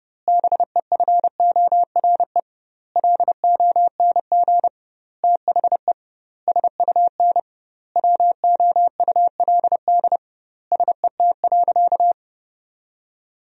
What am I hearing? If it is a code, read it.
Morse code, heard as a keyed tone, about 30 wpm, 700 Hz. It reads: BEFORE LONG THE SUN WOULD SET.
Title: 2x Word-Spacing - Sentences from Top 400 Words - 30wpm